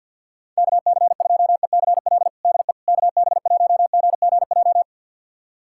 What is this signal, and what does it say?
Morse code, heard as a keyed tone, about 51 wpm, 700 Hz. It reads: KC2ECL DE KB1GGJ
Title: USA Callsign DE Callsign - 6 Characters - 50wpm